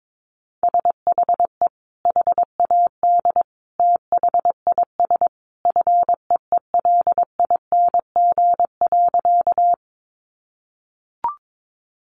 Morse code, heard as a keyed tone, about 22 wpm, 700 Hz. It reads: SHE HAD THIS FEELING.
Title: Sentences from Top 1200 Words 22wpm